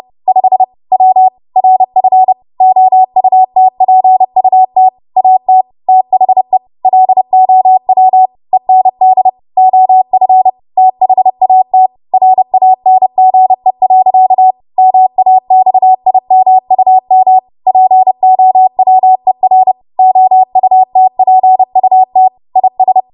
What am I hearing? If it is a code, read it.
Morse code, heard as a keyed tone, about 30 wpm, 750 Hz. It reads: 5 W RF OUTPUT AT THE LOW END OF THAT RANGE. MAXIMUM POWER OUTPUT IS